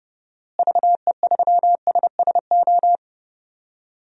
Morse code, heard as a keyed tone, about 30 wpm, 700 Hz. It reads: VE3SSO
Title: DXCC-100 Call Signs - 6 Characters - Repeated 3x - 30wpm